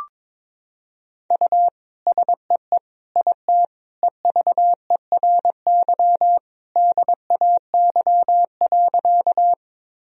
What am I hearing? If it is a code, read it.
Morse code, heard as a keyed tone, about 22 wpm, 700 Hz. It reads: U SEE IT EVERY DAY.
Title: Sentences from Top 200 Words and Abbreviations 22wpm